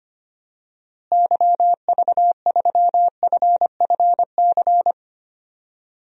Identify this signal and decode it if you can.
Morse code, heard as a keyed tone, about 25 wpm, 700 Hz. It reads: YV3FFC
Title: DXCC-100 Call Signs - 6 Characters - Repeated 3x - 25wpm